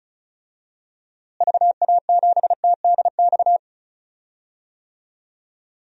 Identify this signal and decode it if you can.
Morse code, heard as a keyed tone, about 36 wpm, 700 Hz. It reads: VA7TDX
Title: DXCC-100 Call Signs - 6 Characters - Repeated 2x - 35wpm